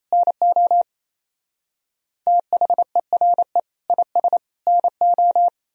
Morse code, heard as a keyed tone, about 34 wpm, 700 Hz. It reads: NO THERE IS NO
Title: Speed-Racing - Sets of 3 Words 28wpm